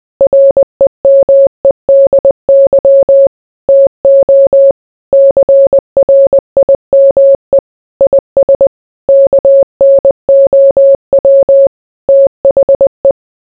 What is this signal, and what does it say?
Morse code, heard as a keyed tone, about 20 wpm, 550 Hz. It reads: REMEDY TO CRIME IS KNOW THE